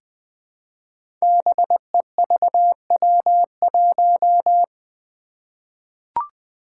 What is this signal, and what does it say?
Morse code, heard as a keyed tone, about 20 wpm, 700 Hz. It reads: BEVW1
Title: Five Challenging Letter-Number Combinations 20wpm